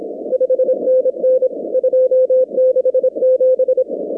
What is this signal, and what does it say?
Morse code, heard as a keyed tone, about 25 wpm, 500 Hz. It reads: 5NN 267
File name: contests / 2019-cw-wpx-kt5j-k5pi- -2330 0-24